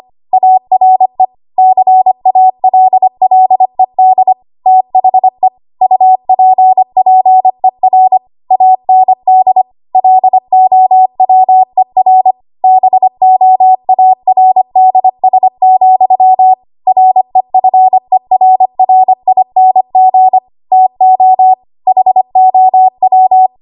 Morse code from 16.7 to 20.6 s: REFERRING